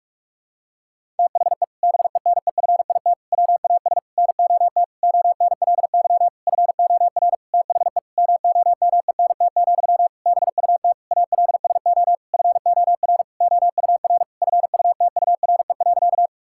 Morse code, heard as a keyed tone, about 46 wpm, 700 Hz. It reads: THE BENEFIT WAS NOT ONLY FOR THE MOMENT, BUT ALSO FOR OUR FUTURE.
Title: Sentences from Top 1000 Words - Review Part 6 - 45wpm